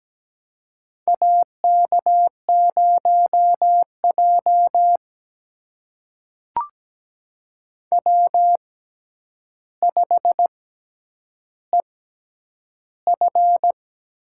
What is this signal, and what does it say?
Morse code, heard as a keyed tone, about 17 wpm, spaced as for 9 wpm, 700 Hz. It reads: AK0J W5EF
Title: USA Call Signs - 4 Characters - 6x Character Spacing - 17wpm